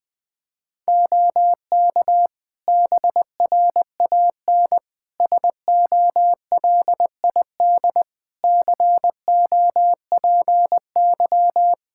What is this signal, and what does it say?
Morse code, heard as a keyed tone, about 20 wpm, 700 Hz. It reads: OK BRAN SOLID COPY
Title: Practice QSO - Exchange 3 - More Info - 20wpm